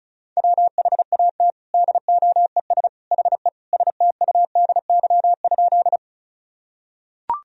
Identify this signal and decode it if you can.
Morse code, heard as a keyed tone, about 34 wpm, 700 Hz. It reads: WHAT DOES HE STUDY?
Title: Sentences from Top 1000 Words - Review Part 1 - 35wpm